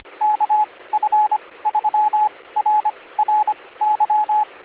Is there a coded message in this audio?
Yes: Morse code from 0.2 to 4.4 s, heard as a keyed tone, about 25 wpm, spaced as for 18 wpm, 850 Hz, with radio noise. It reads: KF3RRY